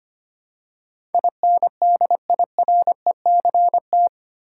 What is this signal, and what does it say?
Morse code, heard as a keyed tone, about 25 wpm, 700 Hz. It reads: INDIRECT
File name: Eight Letter Words - 25wpm